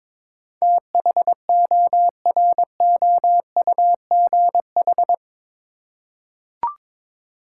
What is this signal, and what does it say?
Morse code, heard as a keyed tone, about 22 wpm, 700 Hz. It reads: THOROUGH
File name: practice / 22wpm / Eight Letter Words - 22wpm